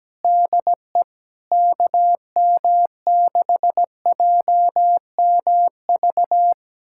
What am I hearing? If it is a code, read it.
Morse code, heard as a keyed tone, about 17 wpm, 700 Hz. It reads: DE KM6JMV
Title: USA Callsign DE Callsign - 6 Characters - 17wpm